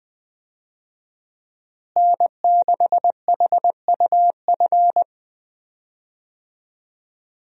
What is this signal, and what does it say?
Morse code, heard as a keyed tone, about 20 wpm, 700 Hz. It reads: N6HUF
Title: USA Call Signs - 5 Characters - Repeated 2x - 20wpm